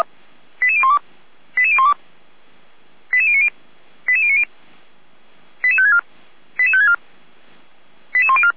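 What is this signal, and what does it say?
Five-tone selective call (ZVEI), heard as quick runs of five tones, tones of about 70 ms. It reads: 88012 88012 88090 88090 88564 88564 88104
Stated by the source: ZVEI five-tone selective call received with an SDR, archived by the Feuerwehr Orlamünde fire brigade.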